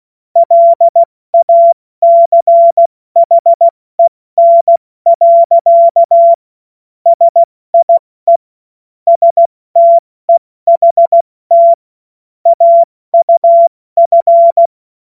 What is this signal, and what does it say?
Morse code, heard as a keyed tone, about 16 wpm, 680 Hz. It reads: LACHEN. SIE STEHT AUF
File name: Morsekurs / Witze / Witz 09 April 2024 70 F80